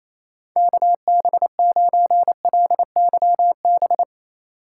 Morse code, heard as a keyed tone, about 27 wpm, 700 Hz. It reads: KB9LYB
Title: USA Call Signs - 6 Characters - 2x Character Spacing - 28wpm